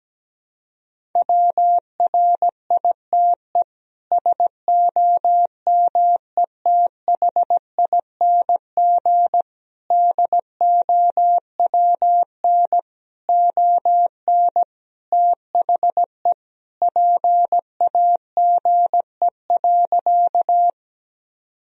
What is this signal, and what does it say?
Morse code, heard as a keyed tone, about 17 wpm, 700 Hz. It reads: WRITE SOMETHING DOWN ON THE PAGE.
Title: Sentences from Top 1000 Words - Review Part 1 - 17wpm